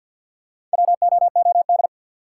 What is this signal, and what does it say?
Morse code, heard as a keyed tone, about 53 wpm, 700 Hz. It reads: WOOD
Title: Sentences from Top 1000 Words - Review Part 2 - New Word Drill - 50wpm